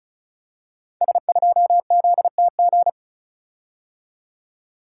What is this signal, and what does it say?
Morse code, heard as a keyed tone, about 36 wpm, 700 Hz. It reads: S2ZTG
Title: Code Groups - 5 Letter:Numbers - 35wpm